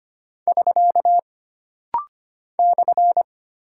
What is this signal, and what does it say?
Morse code, heard as a keyed tone, about 25 wpm, 700 Hz. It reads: <SK> /
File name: Single Punctuation-Prosign - Rapid-Fire - 25wpm